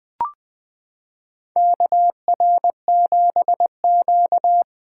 Morse code, heard as a keyed tone, about 21 wpm, 700 Hz. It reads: KR7Q
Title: 2023 SOTA Chaser Call Signs - 20wpm